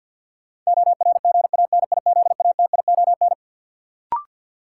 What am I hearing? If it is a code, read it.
Morse code, heard as a keyed tone, about 47 wpm, 700 Hz. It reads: ORGANIZATION
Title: Sentences from Top 1000 Words - Review Part 5 - New Word Drill - 50wpm